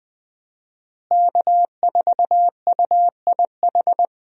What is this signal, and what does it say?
Morse code, heard as a keyed tone, about 20 wpm, 700 Hz. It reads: K4UIH
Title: USA Call Signs - 5 Characters - Repeated 2x - 20wpm